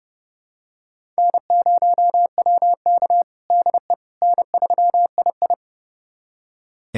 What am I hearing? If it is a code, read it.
Morse code, heard as a keyed tone, about 31 wpm, 700 Hz. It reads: N0WK DE N3II
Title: USA Callsign DE Callsign - 4 Characters - 30wpm